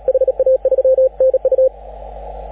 Morse code, heard as a keyed tone, about 35 wpm, 530 Hz, with radio noise. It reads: HA3NU
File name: AUDIO2019 / HG6N CW SSB MIX